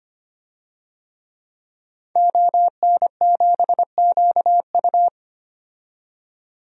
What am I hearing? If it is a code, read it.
Morse code, heard as a keyed tone, about 25 wpm, 700 Hz. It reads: ON7QU
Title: DXCC-100 Call Signs - 5 Characters - 25wpm